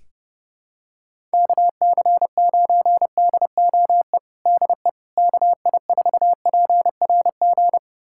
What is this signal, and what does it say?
Morse code, heard as a keyed tone, about 30 wpm, 700 Hz. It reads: KC9DOE DE KI4PRG
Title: USA Callsign DE Callsign - 6 Characters - 30wpm